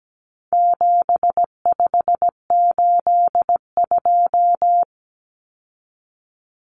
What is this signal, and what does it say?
Morse code, heard as a keyed tone, about 17 wpm, 700 Hz. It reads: 7582
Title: Numbers - 4-Digit 17wpm